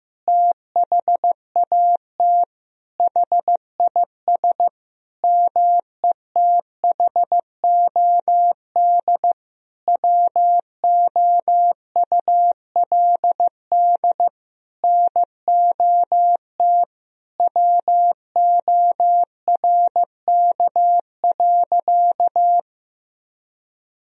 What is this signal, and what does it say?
Morse code, heard as a keyed tone, about 15 wpm, 700 Hz. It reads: THAT HIS METHOD WOULD NOT WORK.
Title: Sentences from Top 1000 Words - Review Part 3 - 15wpm